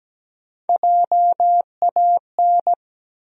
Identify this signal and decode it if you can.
Morse code, heard as a keyed tone, about 17 wpm, 700 Hz. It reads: JAN
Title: USA Common Names - Max 3 Letters - 17wpm